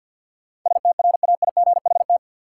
Morse code, heard as a keyed tone, about 51 wpm, 700 Hz. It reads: STRAIGHT
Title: Sentences from Top 1000 Words - New Word Drill - 50wpm